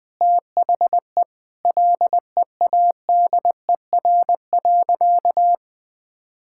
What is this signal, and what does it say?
Morse code, heard as a keyed tone, about 20 wpm, 700 Hz. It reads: THE LEADER.
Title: Sentences with 3-Word Phrases - 20wpm